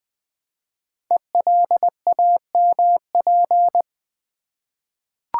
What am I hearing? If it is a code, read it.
Morse code, heard as a keyed tone, about 20 wpm, 700 Hz. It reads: ELAMP
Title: Code Groups - 5 Letters - 20wpm